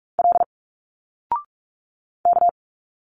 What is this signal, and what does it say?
Morse code, heard as a keyed tone, about 44 wpm, 700 Hz. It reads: L K
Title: Single Letters 45wpm